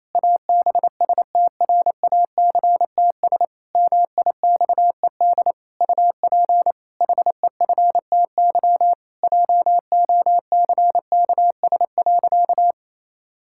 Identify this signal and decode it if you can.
Morse code, heard as a keyed tone, about 27 wpm, 700 Hz. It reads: ABSTRACTS MIXED UP HEFTY JOCKS.
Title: Pangram Challenge v2 - 28wpm